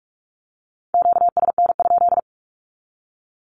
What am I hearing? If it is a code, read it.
Morse code, heard as a keyed tone, about 46 wpm, 700 Hz. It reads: QSN?
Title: Q-Codes with Spoken Meaning - 45wpm